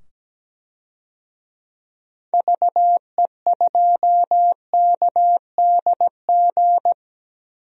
Morse code, heard as a keyed tone, about 17 wpm, 700 Hz. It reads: VE2KDG
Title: DXCC-100 Call Signs - 6 Characters - 17wpm